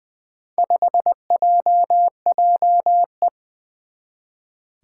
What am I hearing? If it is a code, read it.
Morse code, heard as a keyed tone, about 21 wpm, 700 Hz. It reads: 5JJE
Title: Four Challenging Letter-Number Combinations 20wpm